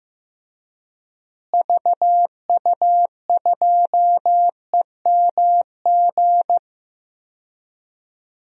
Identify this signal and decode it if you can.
Morse code, heard as a keyed tone, about 15 wpm, 700 Hz. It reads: VU2EMG
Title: DXCC-100 Call Signs - 6 Characters - Repeated 2x - 15wpm